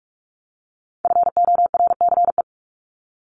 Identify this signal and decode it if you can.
Morse code, heard as a keyed tone, about 44 wpm, 700 Hz. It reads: FORCE
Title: ED Suffix - Top 100 Words with Base Words 45wpm